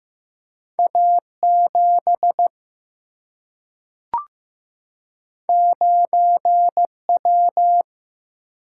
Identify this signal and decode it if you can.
Morse code, heard as a keyed tone, about 15 wpm, 700 Hz. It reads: A 7 9 W
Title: Single-Letter Single-Number 15wpm